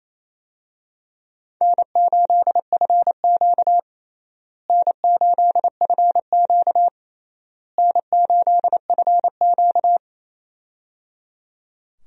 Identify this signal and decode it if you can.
Morse code, heard as a keyed tone, about 28 wpm, 700 Hz. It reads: N8FQ N8FQ N8FQ
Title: USA Call Signs - 4 Characters - Repeated 3x - 28wpm